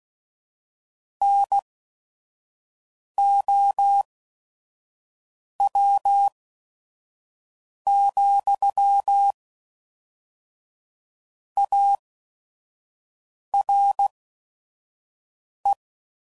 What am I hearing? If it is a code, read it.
Morse code, heard as a keyed tone, about 16 wpm, spaced as for 5 wpm, 800 Hz. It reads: NOW, ARE
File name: CW Code Groups @5WPM